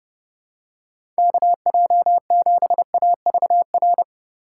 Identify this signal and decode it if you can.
Morse code, heard as a keyed tone, about 30 wpm, 700 Hz. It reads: KJ7AVR
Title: USA Call Signs - 6 Characters - 4x Character Spacing - 30wpm